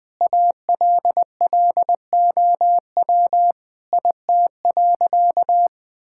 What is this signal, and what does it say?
Morse code, heard as a keyed tone, about 20 wpm, 700 Hz. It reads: ALLOW IT.
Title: Sentences from Top 1000 Words - Review Part 5 - 20wpm